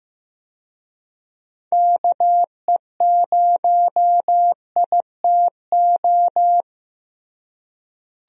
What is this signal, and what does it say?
Morse code, heard as a keyed tone, about 15 wpm, 700 Hz. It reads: KE0ITO
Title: USA Call Signs - 6 Characters - Repeated 2x - 15wpm